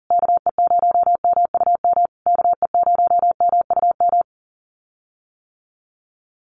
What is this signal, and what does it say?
Morse code, heard as a keyed tone, about 40 wpm, 700 Hz. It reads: KE0MUM KE0MUM
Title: USA Call Signs Repeated 3x 40wpm